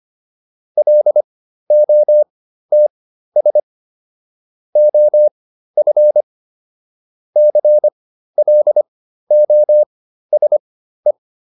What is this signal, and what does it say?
Morse code, heard as a keyed tone, about 25 wpm, spaced as for 13 wpm, 600 Hz. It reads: LOTS OF CLOSE